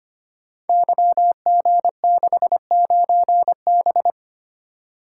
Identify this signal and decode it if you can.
Morse code, heard as a keyed tone, about 25 wpm, 700 Hz. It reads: YG69B
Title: Code Groups - 5 Letter:Numbers - 25wpm